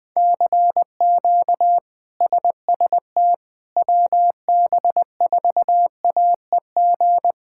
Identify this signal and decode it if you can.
Morse code, heard as a keyed tone, about 20 wpm, 700 Hz. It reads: CQ SST WB4AEG
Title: K1USN SST QSO Practice - 20wpm